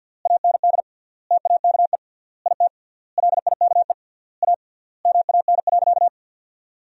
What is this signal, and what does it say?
Morse code, heard as a keyed tone, about 53 wpm, 700 Hz. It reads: AND TAKE IT LIKE A MAN.
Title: 3x Word-Spacing - Sentences from Top 300 Words - 50wpm